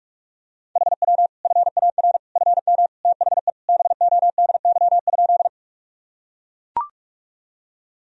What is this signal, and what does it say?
Morse code, heard as a keyed tone, about 41 wpm, 700 Hz. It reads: HW FAR FM THE BODY?
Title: Sentences from Top 400 Words and Abbreviations 45wpm